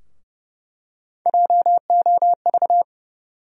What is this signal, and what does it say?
Morse code, heard as a keyed tone, about 30 wpm, 700 Hz. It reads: JOV
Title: USA Call Sign Postfixes - 3 Letters 30wpm